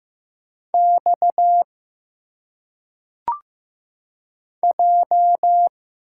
Morse code, heard as a keyed tone, about 15 wpm, 700 Hz. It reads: X J
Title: Single Letters - Phonetic Pronunciation 15wpm